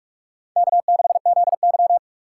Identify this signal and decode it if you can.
Morse code, heard as a keyed tone, about 46 wpm, 700 Hz. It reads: K6ZY
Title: USA Call Signs - 4 Characters - 3x Character Spacing - 45wpm